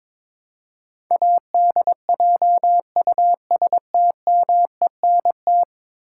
Morse code, heard as a keyed tone, about 22 wpm, 700 Hz. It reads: ADJUSTMENT
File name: practice / 22wpm / 10 Letter Words - First 3 Letters Given - 22wpm